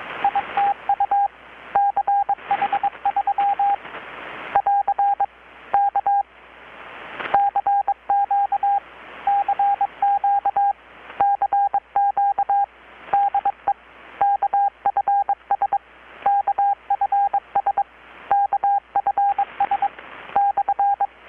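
Morse code, heard as a keyed tone, about 21 wpm, 800 Hz, with radio noise. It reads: ITU CH3 <AR> K CQ CQ CQ DE KFS KFS KFS /
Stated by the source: Morse from KPH's Night of Nights XX maritime radiotelegraphy event, received in Chiapas, Mexico about 3,800 km away on a Kenwood R-600 with a grounded 10 m long-wire antenna.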